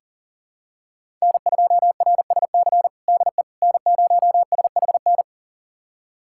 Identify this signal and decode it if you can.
Morse code, heard as a keyed tone, about 40 wpm, 700 Hz. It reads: N2RSC DE N0SHN
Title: USA Callsign DE Callsign - 5 Characters - 40wpm